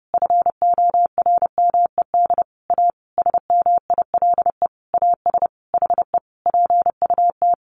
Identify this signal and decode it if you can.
Morse code, heard as a keyed tone, about 30 wpm, 700 Hz. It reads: FORMED A SMILE AS HE PUT
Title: A Sound Of Thunder 30wpm